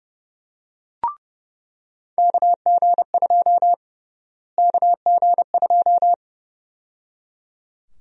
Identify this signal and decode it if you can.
Morse code, heard as a keyed tone, about 32 wpm, 700 Hz. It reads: KG2 KG2
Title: USA Call Sign Prefixes - Repeated 2x - 30wpm